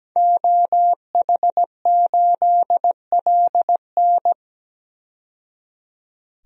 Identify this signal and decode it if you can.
Morse code, heard as a keyed tone, about 17 wpm, 700 Hz. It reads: OH8LN
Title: DXCC-100 Call Signs - 5 Characters - 17wpm